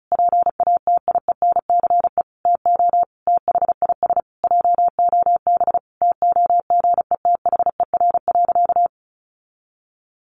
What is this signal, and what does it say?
Morse code, heard as a keyed tone, about 34 wpm, 700 Hz. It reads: PATIENCE TO THIS JOB TOGETHER.
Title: Sentences from Top 1300 Words 35wpm